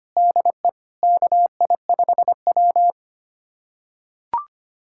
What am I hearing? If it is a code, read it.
Morse code, heard as a keyed tone, about 25 wpm, 700 Hz. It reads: DE KI5W
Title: USA Callsign DE Callsign - 4 Characters - 25wpm